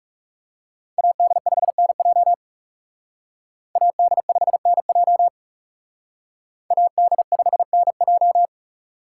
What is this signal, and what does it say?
Morse code, heard as a keyed tone, about 37 wpm, 700 Hz. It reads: AD5NJ AD5NJ AD5NJ
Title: Speed Racing - USA 2x2 Call Signs 30wpm